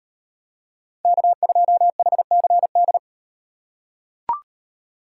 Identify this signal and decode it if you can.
Morse code, heard as a keyed tone, about 38 wpm, 700 Hz. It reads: K2HCD